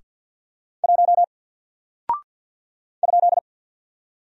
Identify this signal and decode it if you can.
Morse code, heard as a keyed tone, about 50 wpm, 700 Hz. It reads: 1 ?
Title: Single Character (Letter:Number:Punctuation:Prosign) 50wpm